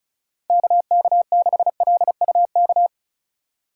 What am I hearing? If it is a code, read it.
Morse code, heard as a keyed tone, about 34 wpm, 700 Hz. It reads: KK6LUK
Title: USA Call Signs - 6 Characters - 2x Character Spacing - 35wpm